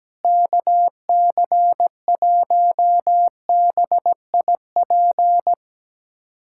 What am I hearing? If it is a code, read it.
Morse code, heard as a keyed tone, about 17 wpm, 700 Hz. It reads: KC1BIP
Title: USA Callsign DE Callsign 17wpm